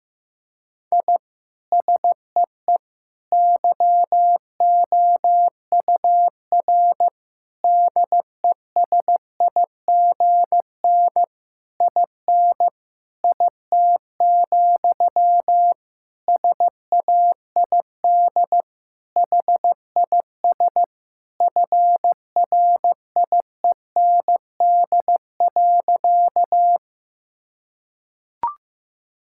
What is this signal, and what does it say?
Morse code, heard as a keyed tone, about 15 wpm, 700 Hz. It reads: I SEE YOUR DESIGN IN IT, SAID HIS FRIEND.
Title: Sentences from Top 1000 Words - Review Part 5 - 15wpm